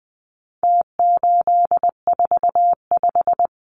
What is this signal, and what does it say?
Morse code, heard as a keyed tone, about 20 wpm, 700 Hz. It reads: T845
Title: Numbers - Contest Style - 4-Digit 20wpm